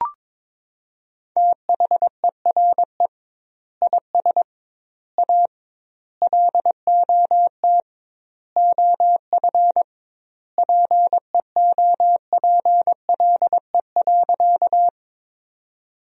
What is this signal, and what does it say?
Morse code, heard as a keyed tone, about 21 wpm, 700 Hz. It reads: THERE IS A LOT OF PEOPLE.
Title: 2x Word-Spacing - Sentences from Top 500 Words - 22wpm